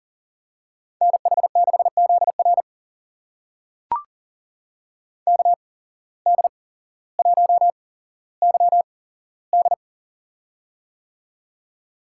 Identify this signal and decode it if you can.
Morse code, heard as a keyed tone, about 39 wpm, spaced as for 17 wpm, 700 Hz. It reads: NH6ZR KD1YD
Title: USA Call Signs - 5 Characters - 8x Character Spacing - 40wpm